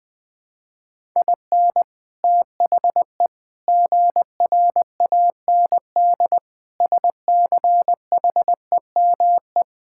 Morse code, heard as a keyed tone, about 21 wpm, 700 Hz. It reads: IN THE GRAND SCHEME
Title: Sets of 6 Words - 20wpm